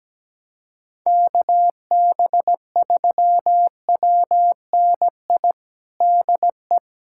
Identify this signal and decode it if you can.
Morse code, heard as a keyed tone, about 17 wpm, 700 Hz. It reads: KB3WNI DE
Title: USA Callsign DE Callsign 17wpm